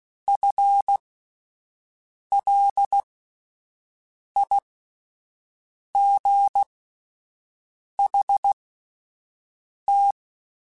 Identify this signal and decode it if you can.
Morse code, heard as a keyed tone, about 16 wpm, spaced as for 5 wpm, 800 Hz. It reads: FLIGHT
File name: Spaced / FL Runs sp mp3s / FL Runs sp @10